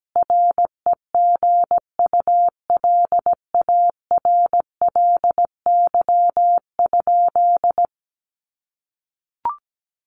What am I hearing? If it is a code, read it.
Morse code, heard as a keyed tone, about 17 wpm, 700 Hz. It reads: REGULARLY?
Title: Sentences from Top 1400 Words 17wpm